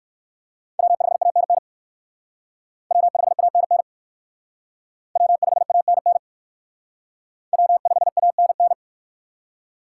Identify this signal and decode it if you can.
Morse code, heard as a keyed tone, about 65 wpm, 700 Hz. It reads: W5ANN W5ANN W5ANN W5ANN